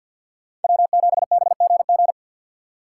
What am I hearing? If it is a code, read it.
Morse code, heard as a keyed tone, about 52 wpm, 690 Hz. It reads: W7BGG
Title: USA Call Signs - 5 Characters - 3x Character Spacing - 50wpm